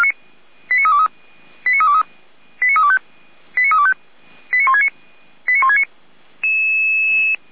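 Five-tone selective call (ZVEI), heard as quick runs of five tones, tones of about 70 ms. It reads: 89323 89323 89325 89325 89169 89169
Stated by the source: ZVEI five-tone selective call received on an SDR.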